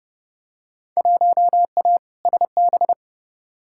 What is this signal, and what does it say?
Morse code, heard as a keyed tone, about 30 wpm, 700 Hz. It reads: 1A SB
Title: Field Day Exchanges Encore - 30wpm